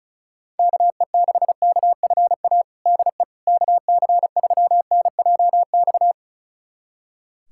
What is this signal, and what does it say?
Morse code, heard as a keyed tone, about 36 wpm, 700 Hz. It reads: KE6KFA DE KC3NJX
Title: USA Callsign DE Callsign - 6 Characters - 35wpm